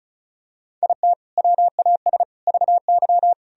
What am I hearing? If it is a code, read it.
Morse code, heard as a keyed tone, about 34 wpm, 700 Hz. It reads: IT WAS VY
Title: Sentences from Top 300 Words and Abbreviations 35wpm